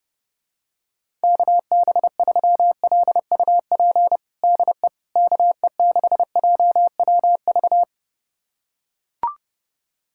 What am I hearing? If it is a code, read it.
Morse code, heard as a keyed tone, about 30 wpm, 700 Hz. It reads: KB3LUP DE KE6JWV
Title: USA Callsign DE Callsign 30wpm